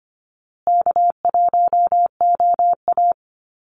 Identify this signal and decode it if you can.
Morse code, heard as a keyed tone, about 25 wpm, 700 Hz. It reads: K1OA
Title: Call Signs Repeated 2x - 25wpm